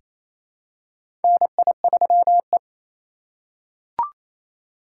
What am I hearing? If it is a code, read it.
Morse code, heard as a keyed tone, about 28 wpm, 700 Hz. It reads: NI3E
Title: USA Call Signs - 4 Characters - 2x Character Spacing - 28wpm